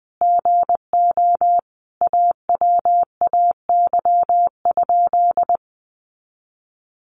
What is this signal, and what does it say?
Morse code, heard as a keyed tone, about 20 wpm, 700 Hz. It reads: GO AWAY?
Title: Sentences from Top 600 Words 20wpm